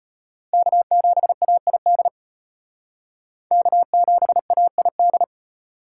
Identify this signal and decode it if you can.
Morse code, heard as a keyed tone, about 33 wpm, 700 Hz. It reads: K7AID K7AID